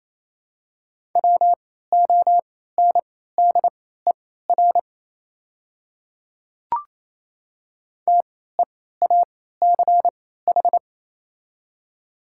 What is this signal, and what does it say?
Morse code, heard as a keyed tone, about 28 wpm, spaced as for 16 wpm, 700 Hz. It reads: WONDER TEACH
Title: 3x Character Spacing - Top 500 Words - 28wpm